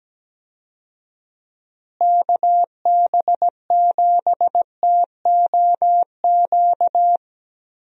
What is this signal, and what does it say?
Morse code, heard as a keyed tone, about 17 wpm, 700 Hz. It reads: KB7TOQ
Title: USA Call Signs - 6 Characters - Repeated 2x - 17wpm